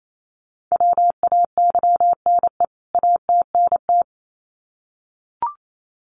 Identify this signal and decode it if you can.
Morse code, heard as a keyed tone, about 28 wpm, 700 Hz. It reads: WAYNE ATNT
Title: CWT Exchange 28wpm